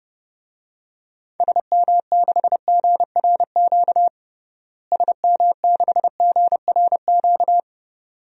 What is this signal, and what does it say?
Morse code, heard as a keyed tone, about 30 wpm, 700 Hz. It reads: SM6GRQ SM6GRQ
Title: DXCC-100 Call Signs - 6 Characters - Repeated 2x - 30wpm